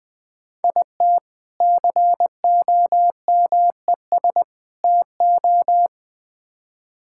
Morse code, heard as a keyed tone, about 21 wpm, 700 Hz. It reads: IT COMES TO
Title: Build A Phrase - 3-Word Phrases - 20wpm